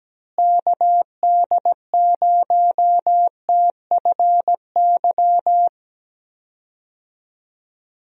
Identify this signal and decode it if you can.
Morse code, heard as a keyed tone, about 17 wpm, 700 Hz. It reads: KD0TFY
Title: USA Call Signs - 6 Characters - Repeated 2x - 17wpm